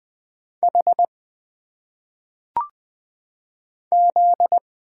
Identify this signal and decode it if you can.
Morse code, heard as a keyed tone, about 20 wpm, 700 Hz. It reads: H Z
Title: Single Letters - Phonetic Pronunciation 20wpm